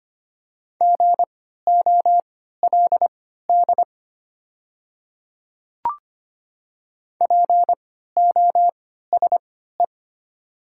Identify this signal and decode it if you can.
Morse code, heard as a keyed tone, about 25 wpm, spaced as for 14 wpm, 700 Hz. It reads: GOLD POSE
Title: 3x Character Spacing - Top 500 Words - 25wpm